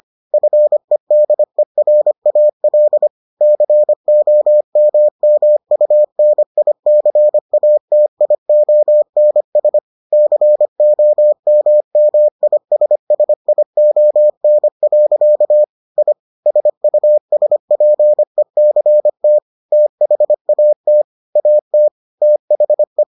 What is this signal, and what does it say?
Morse code, heard as a keyed tone, about 25 wpm, 600 Hz. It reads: FEDERAL COMMUNICATIONS COMMISSION. I SUSPECT THAT AT THE